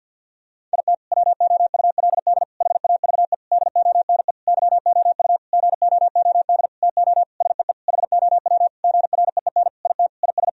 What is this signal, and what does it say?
Morse code, heard as a keyed tone, about 50 wpm, 700 Hz. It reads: IT WOULD HAVE DONE YOU GOOD TO SEE HOW GREEN IT IS
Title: Sentences from Top 1000 Words - Review Part 2 - 50wpm